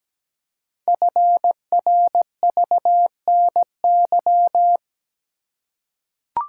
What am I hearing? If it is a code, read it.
Morse code, heard as a keyed tone, about 17 wpm, 700 Hz. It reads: FRVNY
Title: Code Groups - 5 Letters - 17wpm